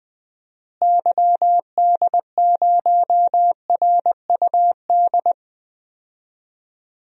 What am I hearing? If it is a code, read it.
Morse code, heard as a keyed tone, about 20 wpm, 700 Hz. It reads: YD0RUD
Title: DXCC-100 Call Signs - 6 Characters - Repeated 2x - 20wpm